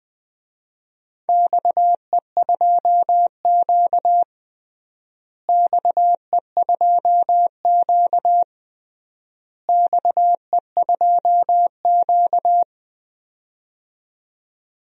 Morse code, heard as a keyed tone, about 21 wpm, 700 Hz. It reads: XE2Q XE2Q XE2Q
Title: DXCC-100 Call Signs - 4 Characters - Repeated 3x - 20wpm